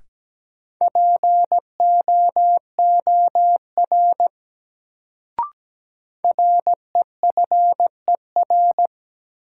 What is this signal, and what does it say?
Morse code, heard as a keyed tone, about 17 wpm, 700 Hz. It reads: POOR REFER
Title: Sentences from Top 1000 Words - New Word Drill - 17wpm